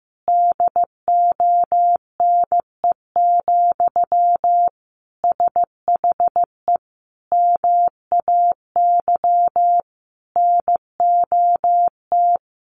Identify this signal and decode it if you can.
Morse code, heard as a keyed tone, about 15 wpm, 700 Hz. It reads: DONE, SHE MAY NOT
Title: Sentences from Top 1200 Words 15wpm